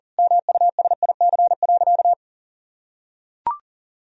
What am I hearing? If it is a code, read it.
Morse code, heard as a keyed tone, about 40 wpm, 700 Hz. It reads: MUSIC.
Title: Sentences from Top 1000 Words - Review Part 1 - 40wpm